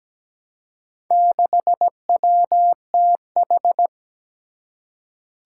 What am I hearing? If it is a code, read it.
Morse code, heard as a keyed tone, about 17 wpm, 700 Hz. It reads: 6WTH
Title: Four Challenging Letter-Number Combinations 17wpm